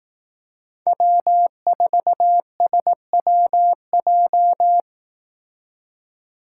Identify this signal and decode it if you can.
Morse code, heard as a keyed tone, about 18 wpm, 700 Hz. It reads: W4SWJ
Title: Speed Racing - USA 1x3 Call Signs 15wpm